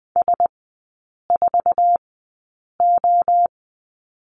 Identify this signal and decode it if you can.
Morse code, heard as a keyed tone, about 20 wpm, spaced as for 8 wpm, 700 Hz. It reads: S4O
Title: 047 Lesson 9e - Number 4 - Three Characters 20wpm